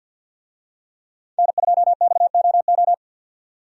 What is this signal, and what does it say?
Morse code, heard as a keyed tone, about 46 wpm, 700 Hz. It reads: N2XOO
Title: USA Call Signs - 5 Characters - 50wpm